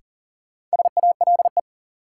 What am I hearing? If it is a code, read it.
Morse code, heard as a keyed tone, about 40 wpm, 700 Hz. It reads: SALE
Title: Sentences from Top 1400 Words - New Word Drill Alt - 40wpm